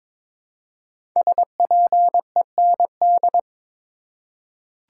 Morse code, heard as a keyed tone, about 22 wpm, 700 Hz. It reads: SPEND
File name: 5 Letter Words - First 3 Letters Given - 22wpm